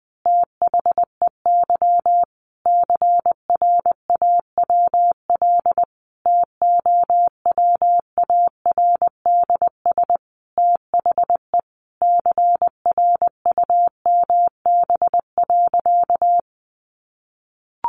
Morse code, heard as a keyed tone, about 20 wpm, 700 Hz. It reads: THEY CRAWL TOWARDS THE CRUMB.
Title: Additional Easy Phrases 20wpm